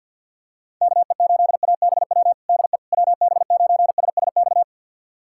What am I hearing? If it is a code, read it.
Morse code, heard as a keyed tone, about 52 wpm, 700 Hz. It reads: KE8ABW DE WB9SSX
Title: USA Callsign DE Callsign - 6 Characters - 50wpm